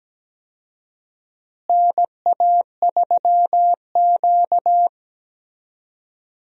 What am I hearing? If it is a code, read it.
Morse code, heard as a keyed tone, about 17 wpm, 700 Hz. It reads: NA3Q
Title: USA Call Signs - 4 Characters - Repeated 2x - 17wpm